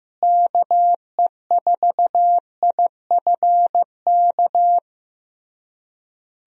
Morse code, heard as a keyed tone, about 15 wpm, 700 Hz. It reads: KE4IFK
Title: USA Call Signs - 6 Characters - Repeated 3x - 15wpm